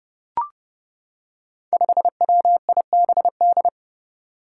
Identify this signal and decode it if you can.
Morse code, heard as a keyed tone, about 30 wpm, 700 Hz. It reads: 5WIBD
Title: Five Challenging Letter-Number Combinations 30wpm